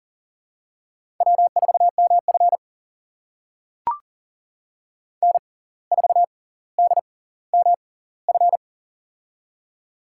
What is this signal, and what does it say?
Morse code, heard as a keyed tone, about 40 wpm, spaced as for 19 wpm, 700 Hz. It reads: W4MF N4DMF
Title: USA Call Signs - 6x Character Spacing - 40wpm